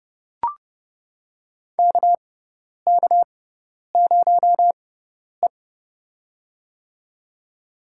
Morse code, heard as a keyed tone, about 30 wpm, spaced as for 10 wpm, 700 Hz. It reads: KK0E
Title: USA Call Signs - 4 Characters - 6x Character Spacing - 30wpm